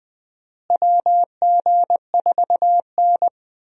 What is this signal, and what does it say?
Morse code, heard as a keyed tone, about 20 wpm, 700 Hz. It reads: WG4N
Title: USA Call Signs - 4 Characters - Repeated 3x - 20wpm